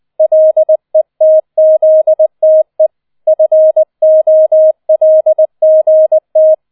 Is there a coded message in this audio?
Yes: Morse code from 0.2 to 6.5 s, heard as a keyed tone, about 20 wpm, 630 Hz. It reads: LETZTE FOLGT